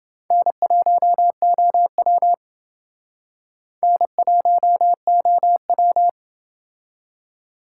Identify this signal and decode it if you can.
Morse code, heard as a keyed tone, about 28 wpm, 700 Hz. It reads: N1OW N1OW